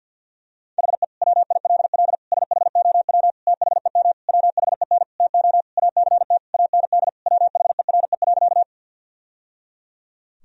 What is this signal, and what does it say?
Morse code, heard as a keyed tone, about 49 wpm, 700 Hz. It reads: HE WILL SHOW THEM WHEN TO ACT AND WHERE.
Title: Sentences from Top 1000 Words - Review Part 1 - 50wpm